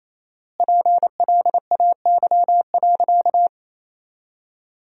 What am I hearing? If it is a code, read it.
Morse code, heard as a keyed tone, about 28 wpm, 700 Hz. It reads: PLAY.
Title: 4x Word-Spacing - Sentences from Top 200 Words - 28wpm